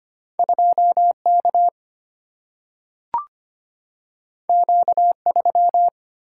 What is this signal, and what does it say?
Morse code, heard as a keyed tone, about 25 wpm, 700 Hz. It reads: 2 K Q 3
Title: Single-Letter Single-Number 25wpm